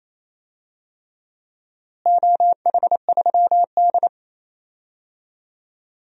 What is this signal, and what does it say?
Morse code, heard as a keyed tone, about 28 wpm, 700 Hz. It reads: OH3D
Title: DXCC-100 Call Signs - 4 Characters - Repeated 3x - 28wpm